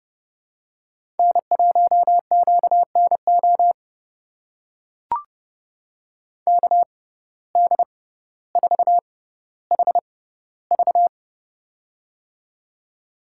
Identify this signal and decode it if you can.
Morse code, heard as a keyed tone, about 30 wpm, spaced as for 15 wpm, 700 Hz. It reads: N1QNO KD4HV
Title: USA Call Signs - 5 Characters - 6x Character Spacing - 30wpm